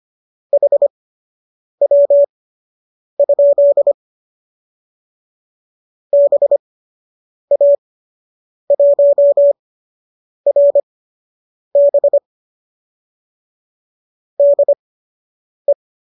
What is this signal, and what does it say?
Morse code, heard as a keyed tone, about 25 wpm, spaced as for 8 wpm, 570 Hz. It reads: HW? BA1RB DE